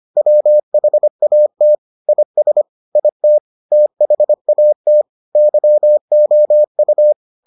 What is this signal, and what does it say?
Morse code, heard as a keyed tone, about 25 wpm, 600 Hz. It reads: WHAT IS IT THAT YOU